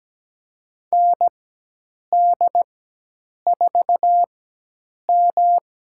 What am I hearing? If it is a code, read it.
Morse code, heard as a keyed tone, about 17 wpm, spaced as for 8 wpm, 700 Hz. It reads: ND4M
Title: USA Call Signs - 4 Characters - 4x Character Spacing - 17wpm